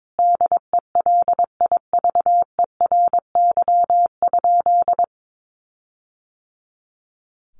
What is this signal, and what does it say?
Morse code, heard as a keyed tone, about 22 wpm, 700 Hz. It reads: DELIVERY?
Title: Sentences from Top 1500 Words 22wpm